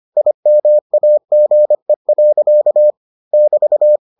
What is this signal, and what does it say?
Morse code, heard as a keyed tone, about 25 wpm, 600 Hz. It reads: IMAGE. <BT>